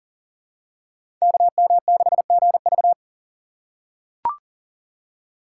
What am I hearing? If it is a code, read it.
Morse code, heard as a keyed tone, about 40 wpm, 700 Hz. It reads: KM6GV
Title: USA Call Signs - 2x Character Spacing - 40wpm